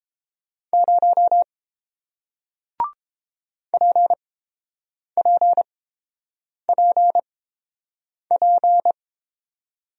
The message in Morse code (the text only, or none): 0 PPPP